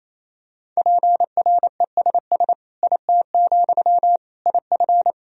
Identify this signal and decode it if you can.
Morse code, heard as a keyed tone, about 28 wpm, 700 Hz. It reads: PRESS IT, IF
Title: Sentences from Top 1000 Words - Review Part 1 - 28wpm